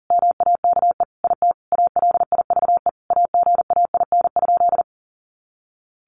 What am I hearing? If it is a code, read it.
Morse code, heard as a keyed tone, about 40 wpm, 700 Hz. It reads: MAKE IT ALIVE AGAIN?
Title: A Sound Of Thunder 40wpm